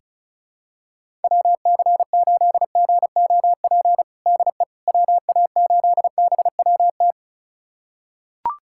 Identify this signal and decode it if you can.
Morse code, heard as a keyed tone, about 34 wpm, 700 Hz. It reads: WC8GOP DE WA8BWT
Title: USA Callsign DE Callsign 35wpm